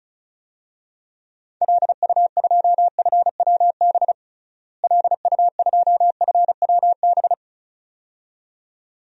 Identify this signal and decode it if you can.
Morse code, heard as a keyed tone, about 36 wpm, 700 Hz. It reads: LU2FWB LU2FWB
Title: DXCC-100 Call Signs - 6 Characters - Repeated 2x - 35wpm